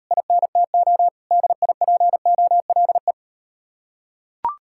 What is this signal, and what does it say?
Morse code, heard as a keyed tone, about 38 wpm, 700 Hz. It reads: INTO DIPOLE